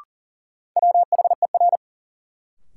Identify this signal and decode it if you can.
Morse code, heard as a keyed tone, about 40 wpm, 700 Hz. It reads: WHER
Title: Sentences from Top 1500 Words - New Word Drill - 40wpm